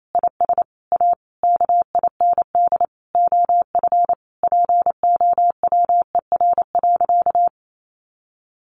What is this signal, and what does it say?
Morse code, heard as a keyed tone, about 28 wpm, 700 Hz. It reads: IS A KIND OF POWER.
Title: Sentences from Top 1300 Words 28wpm